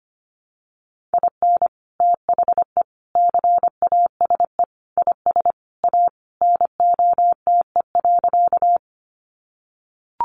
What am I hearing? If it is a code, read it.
Morse code, heard as a keyed tone, about 25 wpm, 700 Hz. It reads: IN THE CASE IS A NOTE.
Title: Sentences from Top 700 Words 25wpm